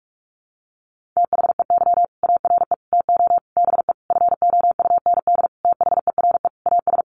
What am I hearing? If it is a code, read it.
Morse code, heard as a keyed tone, about 44 wpm, 700 Hz. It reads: THEY ARE TO BE FOUND THERE AS
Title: Sentences from Top 1400 Words 45wpm